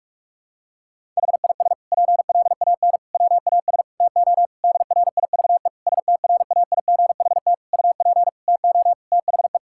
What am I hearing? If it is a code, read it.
Morse code, heard as a keyed tone, about 46 wpm, 700 Hz. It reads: HIS PLAN WAS TO DRIVE STRAIGHT UP TO THE
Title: Sentences from Top 1000 Words - Review Part 6 - 45wpm